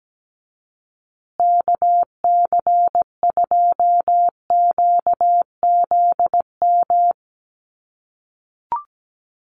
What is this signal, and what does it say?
Morse code, heard as a keyed tone, about 17 wpm, 700 Hz. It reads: KC2QZM
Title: USA Call Signs Repeated 2x 17wpm